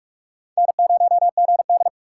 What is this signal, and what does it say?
Morse code, heard as a keyed tone, about 60 wpm, spaced as for 53 wpm, 700 Hz. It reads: N0GD
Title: Speed Racing - USA 1x2 Call Signs 45wpm